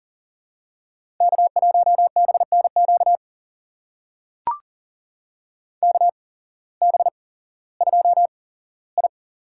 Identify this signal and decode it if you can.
Morse code, heard as a keyed tone, about 40 wpm, spaced as for 19 wpm, 700 Hz. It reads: K1BNQ KB2I
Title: USA Call Signs - 5 Characters - 8x Character Spacing - 40wpm